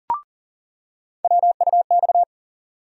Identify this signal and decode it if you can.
Morse code, heard as a keyed tone, about 40 wpm, 700 Hz. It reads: WUX
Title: USA Call Sign Postfixes - 3 Letters 40wpm